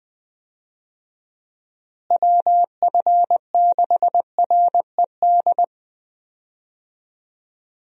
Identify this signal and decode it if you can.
Morse code, heard as a keyed tone, about 20 wpm, 700 Hz. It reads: WF6RED
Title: USA Call Signs - 6 Characters - Repeated 3x - 20wpm